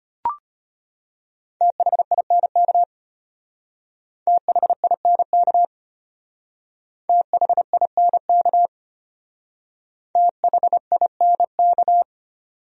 Morse code, heard as a keyed tone, about 31 wpm, 700 Hz. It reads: THINK THINK THINK THINK
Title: Speed-Racing Top 200 Words 25wpm